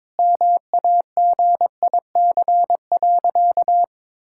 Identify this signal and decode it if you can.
Morse code, heard as a keyed tone, about 22 wpm, 700 Hz. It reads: MAGIC.
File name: practice / 22wpm / Sentences with 3-Word Phrases - 22wpm